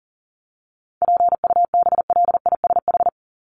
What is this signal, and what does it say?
Morse code, heard as a keyed tone, about 40 wpm, 700 Hz. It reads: PUBLISH
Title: ED Suffix - Top 100 Words with Base Words 40wpm